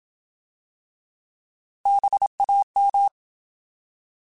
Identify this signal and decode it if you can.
Morse code, heard as a keyed tone, about 27 wpm, 800 Hz. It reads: BAM
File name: Alphabetical / Alphabetical mp3s / Bs @25